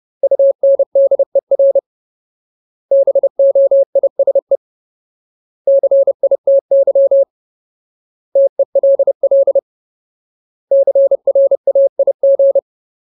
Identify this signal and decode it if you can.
Morse code, heard as a keyed tone, about 30 wpm, 550 Hz. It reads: UNDER BOISE CITY TELL CRAIG
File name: Ultimate Word List / 30WPM Ultimate